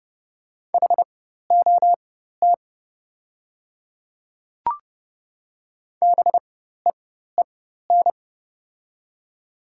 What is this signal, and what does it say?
Morse code, heard as a keyed tone, about 30 wpm, spaced as for 14 wpm, 700 Hz. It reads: HOT BEEN
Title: 4x Character Spacing - Top 100 Words - 30wpm